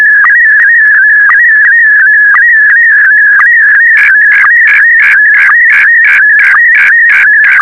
SSTV (slow-scan television), Scottie DX.